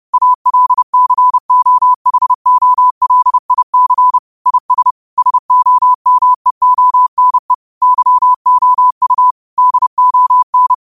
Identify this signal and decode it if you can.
Morse code, heard as a keyed tone, about 30 wpm, 1000 Hz. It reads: ALCOHOLIC IS SOMEONE YOU DON
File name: qotdfiles / 30wpm